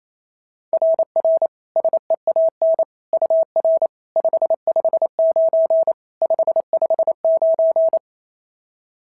Morse code, heard as a keyed tone, about 28 wpm, 660 Hz. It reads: RR SEAN UR 559 559
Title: SOTA QSO Practice 28wpm